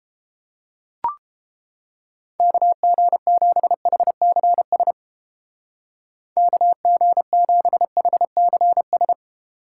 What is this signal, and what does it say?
Morse code, heard as a keyed tone, about 30 wpm, 700 Hz. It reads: KG7HCS KG7HCS